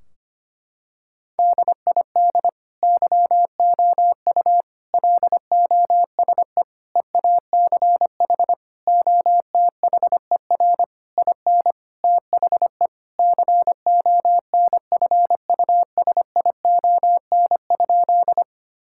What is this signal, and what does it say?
Morse code, heard as a keyed tone, about 25 wpm, 700 Hz. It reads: DID YOU LOSE EACH OTHER IN THE CONFUSION?
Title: Sentences from Top 1600 Words 25wpm